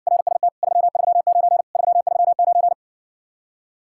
Morse code, heard as a keyed tone, about 61 wpm, 700 Hz. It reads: RST 339 339